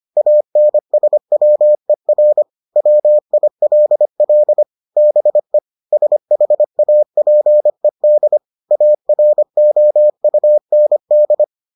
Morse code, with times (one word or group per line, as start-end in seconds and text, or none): 0.2-2.4 ANSWER
2.8-4.6 WILL
5.0-5.6 BE
5.9-8.4 SHAPED
8.7-11.4 AROUND